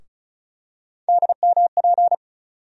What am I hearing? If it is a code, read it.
Morse code, heard as a keyed tone, about 36 wpm, 700 Hz. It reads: DMP
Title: USA Call Sign Postfixes - 3 Letters 35wpm